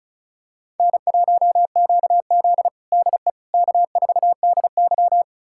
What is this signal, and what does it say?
Morse code, heard as a keyed tone, about 36 wpm, 700 Hz. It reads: N1QZ DE K4DY
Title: USA Callsign DE Callsign - 4 Characters - 35wpm